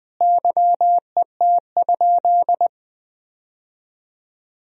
Morse code, heard as a keyed tone, about 20 wpm, 700 Hz. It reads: YET?
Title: 3x Word-Spacing - Sentences from Top 500 Words - 20wpm